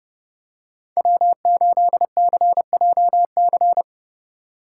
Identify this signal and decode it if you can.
Morse code, heard as a keyed tone, about 30 wpm, 700 Hz. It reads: W8CJC
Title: USA Call Signs - 8x Character Spacing - 30wpm